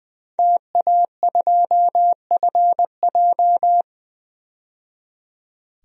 Morse code, heard as a keyed tone, about 20 wpm, 700 Hz. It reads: TA2FJ
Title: DXCC-100 Call Signs - 5 Characters - 20wpm